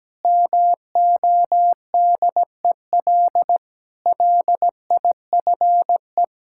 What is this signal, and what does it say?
Morse code, heard as a keyed tone, about 17 wpm, 700 Hz. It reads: MODEL LIFE
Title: Sentences from Top 1000 Words - Review Part 5 - 17wpm